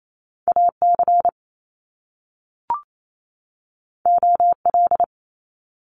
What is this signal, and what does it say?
Morse code, heard as a keyed tone, about 28 wpm, 700 Hz. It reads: AC OL
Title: Top Two Letter Combinations in Words 28wpm